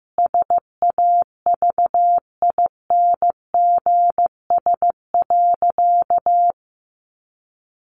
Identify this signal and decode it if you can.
Morse code, heard as a keyed tone, about 15 wpm, 700 Hz. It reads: SAVINGS.
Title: Sentences from Top 1400 Words 15wpm